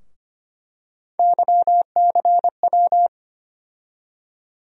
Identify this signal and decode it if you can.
Morse code, heard as a keyed tone, about 25 wpm, 700 Hz. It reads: YCW
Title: USA Call Sign Postfixes - 3 Letters 25wpm